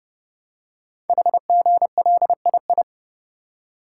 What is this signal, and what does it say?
Morse code, heard as a keyed tone, about 30 wpm, 700 Hz. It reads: HGLII
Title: Code Groups - 5 Letters - 30wpm